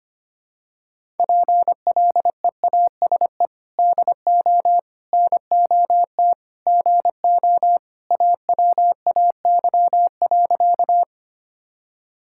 Morse code, heard as a keyed tone, about 25 wpm, 700 Hz. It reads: PLEASE DO NOT GO AWAY.
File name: practice / 25wpm / Sentences with 4-Word Phrases - 25wpm